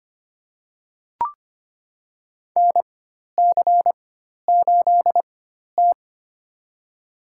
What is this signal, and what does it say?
Morse code, heard as a keyed tone, about 25 wpm, spaced as for 12 wpm, 700 Hz. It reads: NC8T
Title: USA Call Signs - 4x Character Spacing - 25wpm